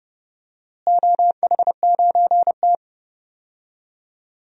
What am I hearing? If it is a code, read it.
Morse code, heard as a keyed tone, about 30 wpm, 700 Hz. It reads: OH9T
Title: DXCC-100 Call Signs - 4 Characters - Repeated 2x - 30wpm